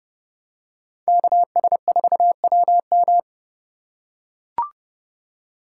Morse code, heard as a keyed tone, about 32 wpm, 700 Hz. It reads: KS4WM
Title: USA Call Signs - 5 Characters - 4x Character Spacing - 30wpm